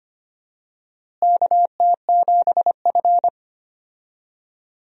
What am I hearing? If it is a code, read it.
Morse code, heard as a keyed tone, about 25 wpm, 700 Hz. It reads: KT7F
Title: USA Call Signs - 4 Characters - Repeated 2x - 25wpm